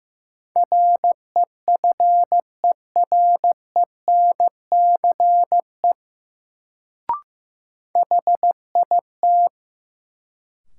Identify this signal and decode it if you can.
Morse code, heard as a keyed tone, about 15 wpm, 700 Hz. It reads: REFERENCE HIT
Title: Sentences from Top 1200 Words - New Word Drill - 15wpm